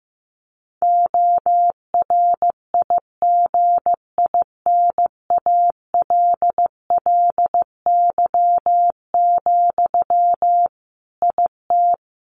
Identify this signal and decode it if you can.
Morse code, heard as a keyed tone, about 15 wpm, 700 Hz. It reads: ORIGINALLY, IT
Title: Sentences from Top 1500 Words 15wpm